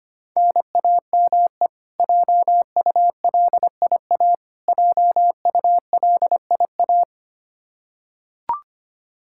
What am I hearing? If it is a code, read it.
Morse code, heard as a keyed tone, about 25 wpm, 700 Hz. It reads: NAME JULIA JULIA
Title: Practice QSO - Exchange 1 - Beginning - 25wpm